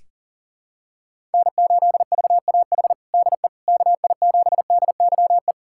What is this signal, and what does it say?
Morse code, heard as a keyed tone, about 40 wpm, 700 Hz. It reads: N8VAH DE KI7DYE
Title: USA Callsign DE Callsign 40wpm